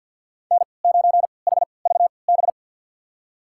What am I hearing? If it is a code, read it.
Morse code, heard as a keyed tone, about 46 wpm, spaced as for 27 wpm, 700 Hz. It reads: N9HVB
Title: USA Call Signs - 3x Character Spacing - 50wpm